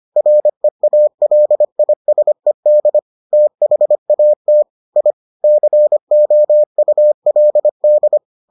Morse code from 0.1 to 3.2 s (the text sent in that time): REALISED